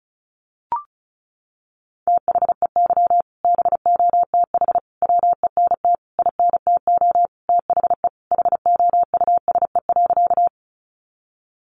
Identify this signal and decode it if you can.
Morse code, heard as a keyed tone, about 34 wpm, 700 Hz. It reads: THEY BOTH WENT INTO THE HOUSE.
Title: Sentences from Top 600 Words 35wpm